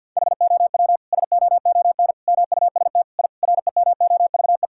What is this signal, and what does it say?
Morse code, heard as a keyed tone, about 51 wpm, 700 Hz. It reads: HOW SOON MUST I REMOVE
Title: Sentences from Top 1000 Words - Review Part 5 - 50wpm